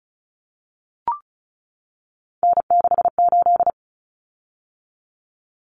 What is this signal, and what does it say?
Morse code, heard as a keyed tone, about 34 wpm, 700 Hz. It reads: N68
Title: Common Cut Numbers - 3-Digit 35wpm